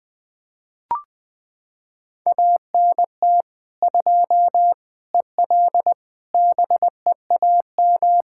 Morse code, heard as a keyed tone, about 20 wpm, 700 Hz. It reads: ANT 2 EL BEAM
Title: Practice QSO - Exchange 2 - Info - 20wpm